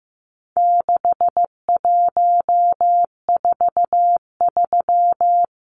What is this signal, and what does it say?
Morse code, heard as a keyed tone, about 15 wpm, 700 Hz. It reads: 6143
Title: Numbers - 4-Digit 15wpm